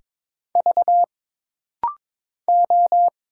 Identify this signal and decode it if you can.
Morse code, heard as a keyed tone, about 22 wpm, 700 Hz. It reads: V O
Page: This practice set is composed of Letters, Numbers, common punctuations, and common prosigns.